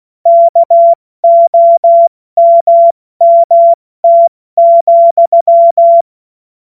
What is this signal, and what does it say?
Morse code, heard as a keyed tone, about 16 wpm, 680 Hz. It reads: KOMMT,